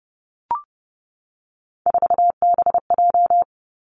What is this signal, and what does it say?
Morse code, heard as a keyed tone, about 30 wpm, 700 Hz. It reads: 4BJ
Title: Three Challenging Letter-Number Combinations 30wpm